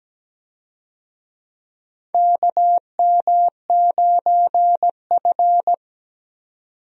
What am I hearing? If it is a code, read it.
Morse code, heard as a keyed tone, about 17 wpm, 700 Hz. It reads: KM9F
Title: USA Call Signs - 4 Characters - 17wpm